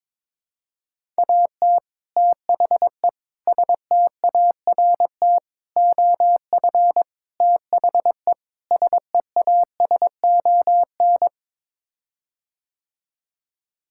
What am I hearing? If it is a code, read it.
Morse code, heard as a keyed tone, about 22 wpm, 700 Hz. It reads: AT THE START OF THE SEASON
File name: practice / 22wpm / Sets of 6 Words - 22wpm